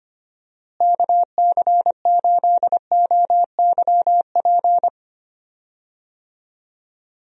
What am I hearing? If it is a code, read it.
Morse code, heard as a keyed tone, about 25 wpm, 700 Hz. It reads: KC8OYP
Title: USA Call Signs - 6 Characters - Repeated 2x - 25wpm